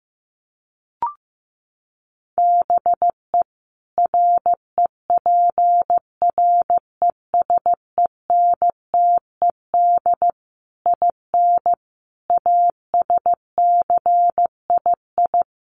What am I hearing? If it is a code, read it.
Morse code, heard as a keyed tone, about 15 wpm, 700 Hz. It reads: BE REPRESENTED IN ASCII
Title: Sets of 5 Words 15wpm